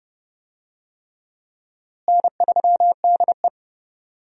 Morse code, heard as a keyed tone, about 30 wpm, 700 Hz. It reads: N3DE
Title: USA Call Signs - 4 Characters - Repeated 3x - 30wpm